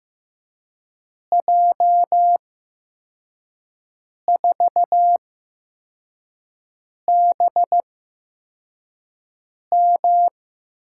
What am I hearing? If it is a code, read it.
Morse code, heard as a keyed tone, about 15 wpm, spaced as for 4 wpm, 700 Hz. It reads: J4BM
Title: USA Call Signs - 8x Character Spacing - 15wpm